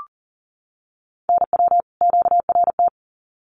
Morse code, heard as a keyed tone, about 40 wpm, 700 Hz. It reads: NW QRT
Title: Common QSO Phrases 40wpm